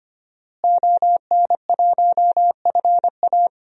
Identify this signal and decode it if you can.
Morse code, heard as a keyed tone, about 25 wpm, 700 Hz. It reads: ON1FA
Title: DXCC-100 Call Signs - 5 Characters - Repeated 2x - 25wpm